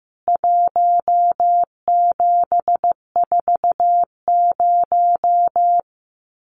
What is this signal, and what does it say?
Morse code, heard as a keyed tone, about 15 wpm, 700 Hz. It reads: 1740
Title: CWT Exchange 15wpm